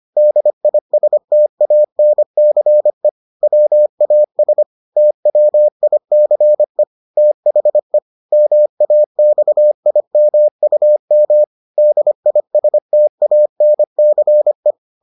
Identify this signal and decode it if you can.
Morse code, heard as a keyed tone, about 25 wpm, 600 Hz. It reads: DISTANCE WAS TWICE THE MAXIMUM DISTANCE